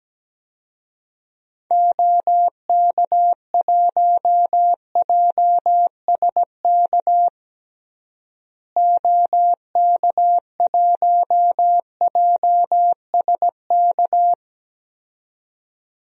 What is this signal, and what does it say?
Morse code, heard as a keyed tone, about 17 wpm, 700 Hz. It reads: OK1JSK OK1JSK
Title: DXCC-100 Call Signs - 6 Characters - Repeated 2x - 17wpm